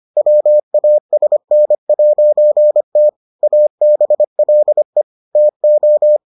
Morse code, heard as a keyed tone, about 25 wpm, 600 Hz. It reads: WASN'T ABLE TO